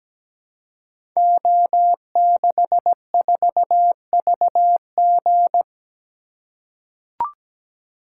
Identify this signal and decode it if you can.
Morse code, heard as a keyed tone, about 17 wpm, 700 Hz. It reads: O64VG
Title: Code Groups - 5 Letter:Numbers - 17wpm